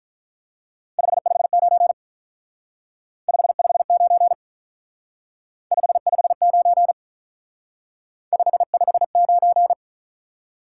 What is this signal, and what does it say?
Morse code, heard as a keyed tone, about 45 wpm, 700 Hz. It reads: 559 559 559 559